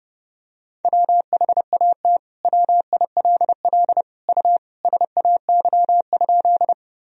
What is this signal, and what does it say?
Morse code, heard as a keyed tone, about 30 wpm, 700 Hz. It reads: WHAT WILL U SAY?
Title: Sentences from Top 200 Words and Abbreviations 30wpm